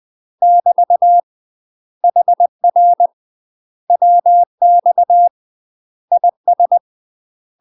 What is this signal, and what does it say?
Morse code, heard as a keyed tone, about 20 wpm, 700 Hz. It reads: <BT> HR WX IS
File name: UK MASTER QSOs TRAINING / 20WPM / UK MASTER QSOs 0066